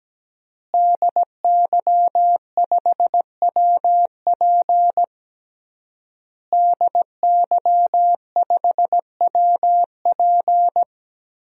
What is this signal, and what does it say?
Morse code, heard as a keyed tone, about 17 wpm, 700 Hz. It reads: DY5WP DY5WP
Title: DXCC-100 Call Signs - 5 Characters - Repeated 3x - 17wpm